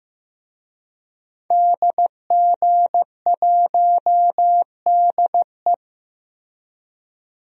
Morse code, heard as a keyed tone, about 15 wpm, 700 Hz. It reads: DG1DE
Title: DXCC-100 Call Signs - 5 Characters - Repeated 3x - 15wpm